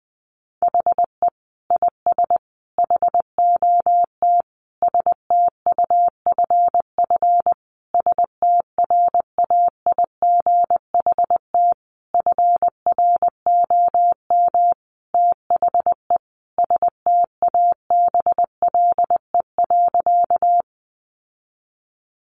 Morse code, heard as a keyed tone, about 20 wpm, 700 Hz. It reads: HE IS HOT STUFF STRAIGHT FROM THE STABLE.
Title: Sentences from Top 1500 Words 20wpm